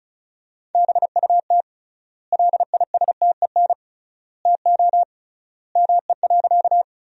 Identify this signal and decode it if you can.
Morse code, heard as a keyed tone, about 36 wpm, 700 Hz. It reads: BUT LISTEN TO ME.
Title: 3x Word-Spacing - Sentences from Top 400 Words - 35wpm